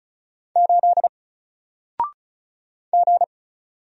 Morse code, heard as a keyed tone, about 35 wpm, 700 Hz. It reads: 8 G
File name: Single-Letter-Number-35wpm